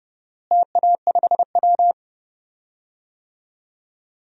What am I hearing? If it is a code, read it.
Morse code, heard as a keyed tone, about 31 wpm, 700 Hz. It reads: TA5W
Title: DXCC-100 Call Signs - 4 Characters - Repeated 2x - 30wpm